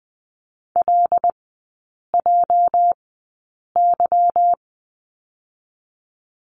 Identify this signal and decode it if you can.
Morse code, heard as a keyed tone, about 20 wpm, spaced as for 8 wpm, 700 Hz. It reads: LJY
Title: 218 Lesson 35e - Number 0 - Three Characters 20wpm